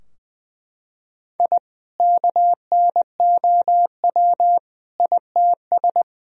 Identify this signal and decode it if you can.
Morse code, heard as a keyed tone, about 20 wpm, 700 Hz. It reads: I KNOW ITS
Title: Sentences from Top 1000 Words - Review Part 2 - 20wpm